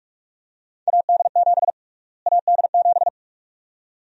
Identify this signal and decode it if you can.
Morse code, heard as a keyed tone, about 46 wpm, 700 Hz. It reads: AD7 AD7
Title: USA Call Sign Prefixes - Repeated 2x - 45wpm